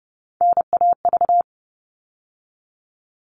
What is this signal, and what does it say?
Morse code, heard as a keyed tone, about 30 wpm, 700 Hz. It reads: NAV
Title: All Cut Numbers - 3-Digit 30wpm